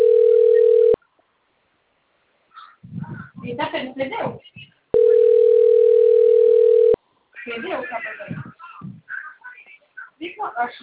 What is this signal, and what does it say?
Ringback tone (North America), 440 + 480 Hz, with speech.
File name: rg-1600-unknown- - - .78009